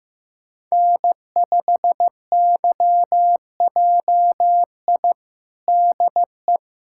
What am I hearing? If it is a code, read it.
Morse code, heard as a keyed tone, about 15 wpm, 700 Hz. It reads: N5YJI DE
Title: USA Callsign DE Callsign - 5 Characters - 15wpm